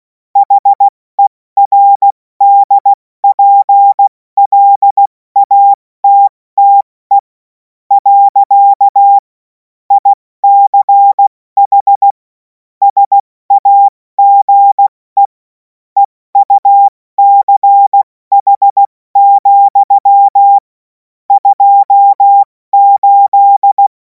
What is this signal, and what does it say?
Morse code, heard as a keyed tone, about 16 wpm, spaced as for 14 wpm, 800 Hz. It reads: HERDPLATTE. ICH SAGE EUCH, 28
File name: Morsekurs / Witze / Witz 11. August 2026 70 F80